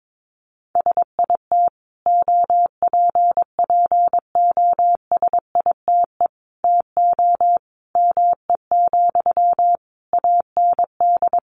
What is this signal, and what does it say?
Morse code, heard as a keyed tone, about 22 wpm, 700 Hz. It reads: SIT OPPOSITE TO ME, AND
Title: Sentences from Top 1300 Words 22wpm